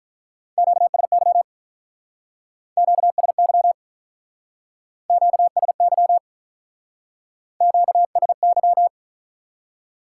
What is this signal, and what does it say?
Morse code, heard as a keyed tone, about 41 wpm, 700 Hz. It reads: QSY QSY QSY QSY
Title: Speed-Racing Top 180 QSO Elements 35wpm